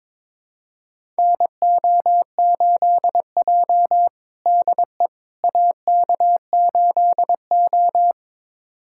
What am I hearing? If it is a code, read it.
Morse code, heard as a keyed tone, about 22 wpm, 700 Hz. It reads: NO8J DE AK8O
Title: USA Callsign DE Callsign - 4 Characters - 22wpm